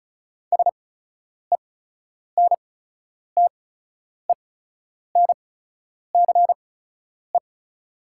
Morse code, heard as a keyed tone, about 36 wpm, spaced as for 10 wpm, 690 Hz. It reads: SENTENCE
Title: 8x Character Spacing - Top 200 Words - 35wpm